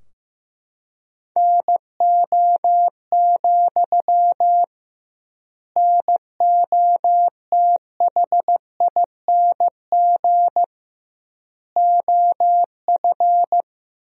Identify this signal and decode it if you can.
Morse code, heard as a keyed tone, about 16 wpm, 700 Hz. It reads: NO, NOTHING OF
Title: 2x Word-Spacing - Sentences from Top 500 Words - 15wpm